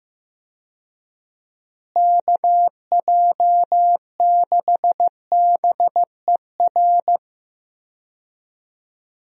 Morse code, heard as a keyed tone, about 15 wpm, 700 Hz. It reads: KJ6BER
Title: USA Call Signs - 6 Characters - Repeated 3x - 15wpm